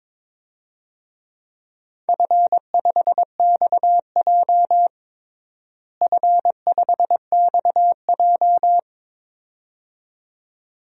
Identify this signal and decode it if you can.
Morse code, heard as a keyed tone, about 23 wpm, 700 Hz. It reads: F5XJ F5XJ
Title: DXCC-100 Call Signs - 4 Characters - Repeated 2x - 22wpm